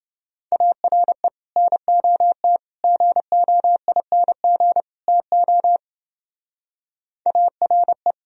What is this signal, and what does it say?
Morse code, heard as a keyed tone, about 30 wpm, 700 Hz. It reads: ARE NOT GOING TO ARE